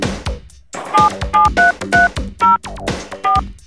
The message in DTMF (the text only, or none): 77330 4